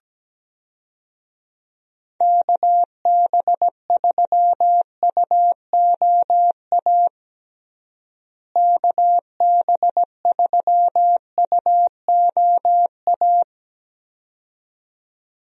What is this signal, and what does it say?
Morse code, heard as a keyed tone, about 17 wpm, 700 Hz. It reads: KB3UOA KB3UOA
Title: USA Call Signs - 6 Characters - Repeated 2x - 17wpm